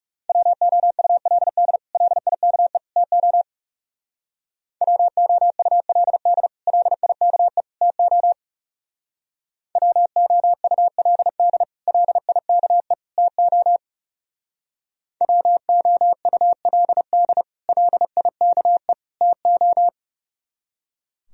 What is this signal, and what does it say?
Morse code, heard as a keyed tone, about 38 wpm, 700 Hz. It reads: WOULD LIKE TO WOULD LIKE TO WOULD LIKE TO WOULD LIKE TO